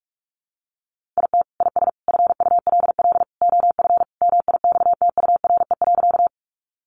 Morse code, heard as a keyed tone, about 44 wpm, 700 Hz. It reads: IT IS FULL OF MIXTURE.
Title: Sentences from Top 1500 Words 45wpm